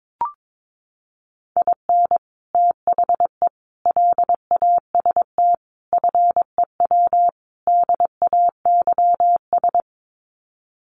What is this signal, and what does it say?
Morse code, heard as a keyed tone, about 22 wpm, 700 Hz. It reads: IN THE LAST FEW DAYS
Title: Sets of 5 Words Encore2 22wpm